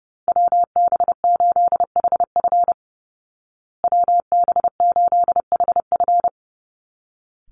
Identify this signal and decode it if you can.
Morse code, heard as a keyed tone, about 30 wpm, 700 Hz. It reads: WB8HF WB8HF
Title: Call Signs Repeated 3x - 30wpm